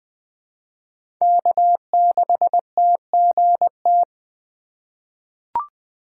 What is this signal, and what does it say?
Morse code, heard as a keyed tone, about 20 wpm, 700 Hz. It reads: K6TGT
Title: USA Call Signs - 5 Characters - 3x Character Spacing - 20wpm